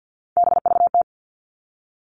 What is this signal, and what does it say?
Morse code, heard as a keyed tone, about 50 wpm, 700 Hz. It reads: BVT
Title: All Cut Numbers - 3-Digit 50wpm